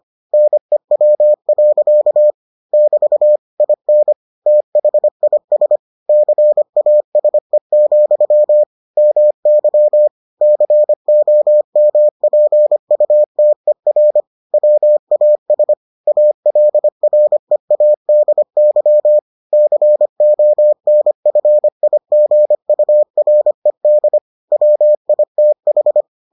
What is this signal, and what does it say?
Morse code, heard as a keyed tone, about 25 wpm, 600 Hz. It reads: NEW. <BT> IN THIS CASE, MY COMPUTER WAS ALREADY CONFIGURED WITH